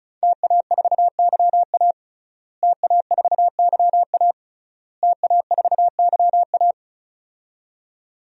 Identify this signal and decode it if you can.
Morse code, heard as a keyed tone, about 36 wpm, 700 Hz. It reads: TA4YA TA4YA TA4YA
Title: DXCC-100 Call Signs - 5 Characters - Repeated 3x - 35wpm